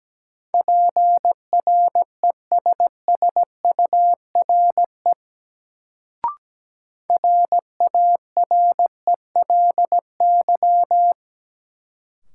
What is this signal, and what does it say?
Morse code, heard as a keyed tone, about 17 wpm, 700 Hz. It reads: PRESSURE RARELY
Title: Sentences from Top 1400 Words - New Word Drill Alt - 17wpm